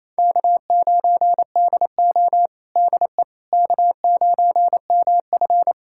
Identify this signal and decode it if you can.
Morse code, heard as a keyed tone, about 27 wpm, 690 Hz. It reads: K9DO DE K9MF
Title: USA Callsign DE Callsign - 4 Characters - 28wpm